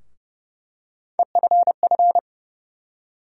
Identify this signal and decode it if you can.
Morse code, heard as a keyed tone, about 30 wpm, 700 Hz. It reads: EFF
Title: USA Call Sign Postfixes - 3 Letters 30wpm